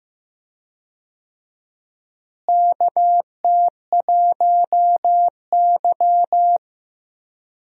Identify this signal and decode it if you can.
Morse code, heard as a keyed tone, about 15 wpm, 700 Hz. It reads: KT1Y
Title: USA Call Signs - 4 Characters - Repeated 2x - 15wpm